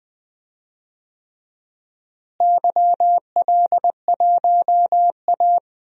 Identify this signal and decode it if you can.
Morse code, heard as a keyed tone, about 20 wpm, 700 Hz. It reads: YL1A
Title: DXCC-100 Call Signs - 4 Characters - Repeated 3x - 20wpm